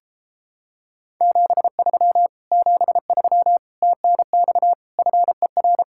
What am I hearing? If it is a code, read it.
Morse code, heard as a keyed tone, about 29 wpm, 700 Hz. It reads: 73 73 TNX FER